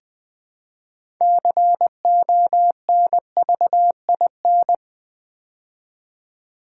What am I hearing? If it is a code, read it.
Morse code, heard as a keyed tone, about 20 wpm, 700 Hz. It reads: CONVIN
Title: Build A Word - 2 Letters At A Time - 8-Letter Words - 20wpm